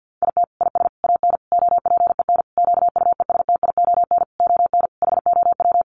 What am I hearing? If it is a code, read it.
Morse code, heard as a keyed tone, about 50 wpm, 700 Hz. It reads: IT IS AN OPEN QUESTION ON HOW
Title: Sentences from Top 1300 Words 50wpm